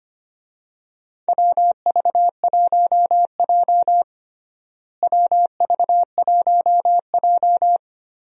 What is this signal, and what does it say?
Morse code, heard as a keyed tone, about 25 wpm, 700 Hz. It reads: WV1J WV1J
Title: USA Call Signs - 4 Characters - Repeated 2x - 25wpm